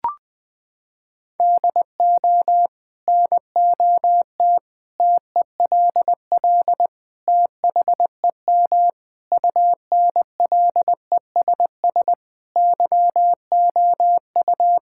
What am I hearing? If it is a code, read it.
Morse code, heard as a keyed tone, about 20 wpm, 700 Hz. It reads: DO NOT TELL THEM UNLESS YOU
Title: Sentences from Top 1000 Words - Review Part 6 - 20wpm